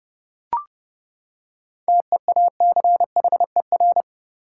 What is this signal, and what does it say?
Morse code, heard as a keyed tone, about 30 wpm, 700 Hz. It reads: TEACHER
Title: Seven Letter Words - 30wpm